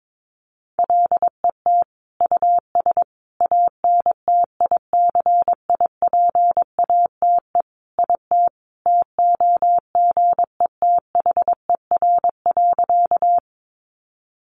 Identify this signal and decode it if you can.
Morse code, heard as a keyed tone, about 22 wpm, 700 Hz. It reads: LET US ANTICIPATE IT TOGETHER.
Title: Sentences from Top 1500 Words 22wpm